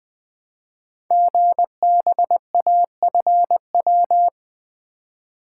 Morse code, heard as a keyed tone, about 20 wpm, 700 Hz. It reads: GBAFW
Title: Code Groups - 5 Letters - 20wpm